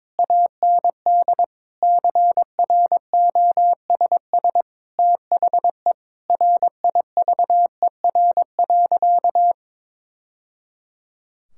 Morse code, heard as a keyed tone, about 22 wpm, 700 Hz. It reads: AND CROSS THE RIVER.
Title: Sentences from Top 1000 Words - Review Part 6 - 22wpm